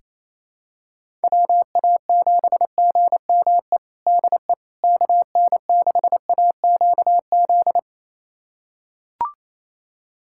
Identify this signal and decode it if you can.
Morse code, heard as a keyed tone, about 28 wpm, 700 Hz. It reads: WA7GME DE KN6AQZ
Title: USA Callsign DE Callsign 28wpm